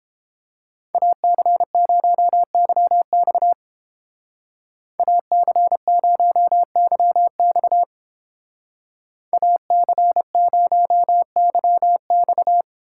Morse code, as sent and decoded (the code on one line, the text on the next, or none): .- -.-. ----- -.-- -..- / .- -.-. ----- -.-- -..- / .- -.-. ----- -.-- -..-
AC0YX AC0YX AC0YX